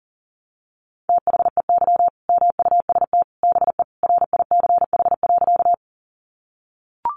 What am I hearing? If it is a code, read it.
Morse code, heard as a keyed tone, about 40 wpm, 700 Hz. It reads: THEY MUST BE RICH.
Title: Sentences from Top 1100 Words 40wpm